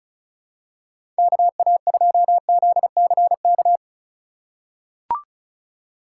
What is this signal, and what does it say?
Morse code, heard as a keyed tone, about 36 wpm, 700 Hz. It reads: KA2ZCK
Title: USA Call Signs - 6 Characters - 3x Character Spacing - 35wpm